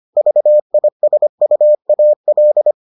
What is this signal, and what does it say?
Morse code, heard as a keyed tone, about 25 wpm, 600 Hz. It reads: VISUAL